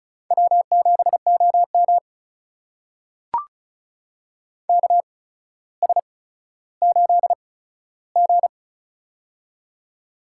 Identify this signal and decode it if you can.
Morse code, heard as a keyed tone, about 36 wpm, spaced as for 15 wpm, 700 Hz. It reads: W7OM KS8G
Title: USA Call Signs - 4 Characters - 8x Character Spacing - 35wpm